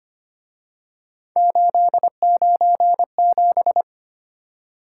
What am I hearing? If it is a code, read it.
Morse code, heard as a keyed tone, about 29 wpm, 700 Hz. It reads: 897
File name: Speed-Racing - Three Letter-Number Combinations - 25wpm